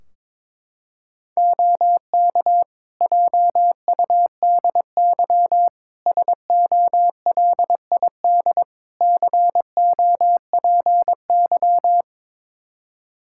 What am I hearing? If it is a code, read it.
Morse code, heard as a keyed tone, about 21 wpm, 700 Hz. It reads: OK JUDY SOLID COPY
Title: Practice QSO - Exchange 3 - More Info - 22wpm